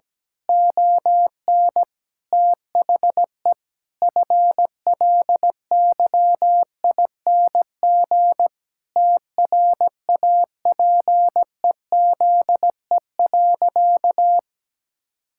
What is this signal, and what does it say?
Morse code, heard as a keyed tone, about 17 wpm, 700 Hz. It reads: ON THE FLYING TRAPEZE.
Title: Pangram Challenge v2 - 17wpm